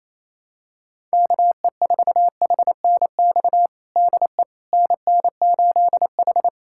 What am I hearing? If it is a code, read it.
Morse code, heard as a keyed tone, about 28 wpm, 700 Hz. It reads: KE4HNX DE NN8H
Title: USA Callsign DE Callsign 28wpm